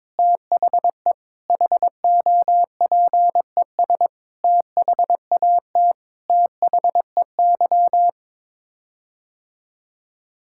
Morse code, heard as a keyed tone, about 21 wpm, 700 Hz. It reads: THE HOPES THAT THEY
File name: Sets of 9 Words - 22wpm